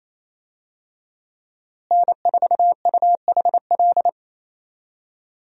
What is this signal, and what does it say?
Morse code, heard as a keyed tone, about 28 wpm, 700 Hz. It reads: N4UHL
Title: USA Call Signs - 5 Characters - Repeated 3x - 28wpm